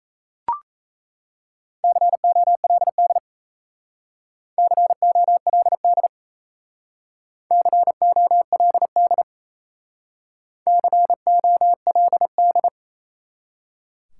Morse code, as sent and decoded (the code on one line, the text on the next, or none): -.-. --- .-.. -.. / -.-. --- .-.. -.. / -.-. --- .-.. -.. / -.-. --- .-.. -..
COLD COLD COLD COLD